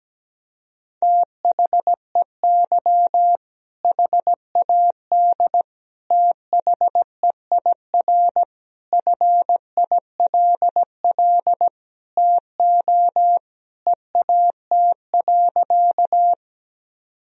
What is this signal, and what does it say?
Morse code, heard as a keyed tone, about 17 wpm, 700 Hz. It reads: THEY HAD THEIR FILL TO EAT.
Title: Sentences from Top 1000 Words - Review Part 3 - 17wpm